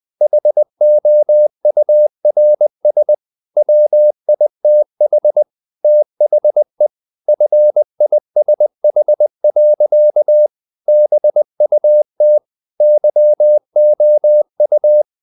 Morse code, heard as a keyed tone, about 20 wpm, 600 Hz. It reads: HOURS WITH THE FISH. BUT YOU
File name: the old man and the sea / the old man and the sea part 1 20wpm